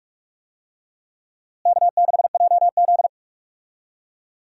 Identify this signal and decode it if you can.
Morse code, heard as a keyed tone, about 44 wpm, 700 Hz. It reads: K6JZ
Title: USA Call Signs - 4 Characters - Repeated 2x - 45wpm